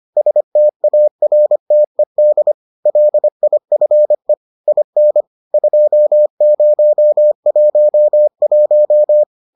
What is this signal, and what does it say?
Morse code, heard as a keyed tone, about 25 wpm, 600 Hz. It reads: STARTED LIFE IN 2011